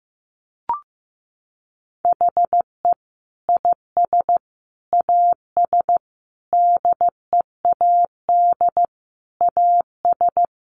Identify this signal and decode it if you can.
Morse code, heard as a keyed tone, about 14 wpm, 700 Hz. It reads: HE IS AS DEAD AS
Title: Sentences from Top 1500 Words 15wpm